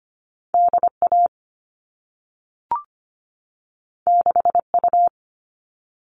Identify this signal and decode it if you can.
Morse code, heard as a keyed tone, about 25 wpm, 700 Hz. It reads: DA 6U
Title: All Cut Numbers - 2-Digit 25wpm